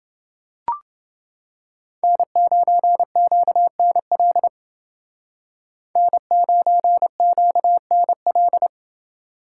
Morse code, heard as a keyed tone, about 29 wpm, 700 Hz. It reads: N9QNL N9QNL